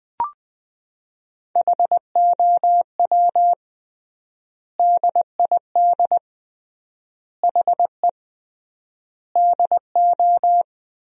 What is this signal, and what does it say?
Morse code, heard as a keyed tone, about 21 wpm, 700 Hz. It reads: HOW DID HE DO
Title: 3x Word-Spacing - Sentences from Top 100 Words - 20wpm